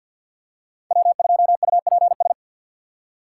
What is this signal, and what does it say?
Morse code, heard as a keyed tone, about 50 wpm, 700 Hz. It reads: W2UPS
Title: USA Call Signs - 4x Character Spacing - 50wpm